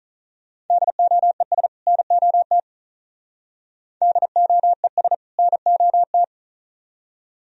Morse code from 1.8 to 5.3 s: NOT DOES

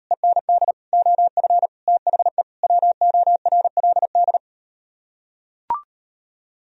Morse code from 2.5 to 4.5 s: WORLD

